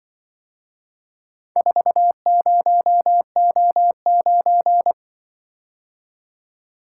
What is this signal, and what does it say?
Morse code, heard as a keyed tone, about 24 wpm, 700 Hz. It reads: 40O9
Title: Speed-Racing - Four Letter-Number Combinations - 20wpm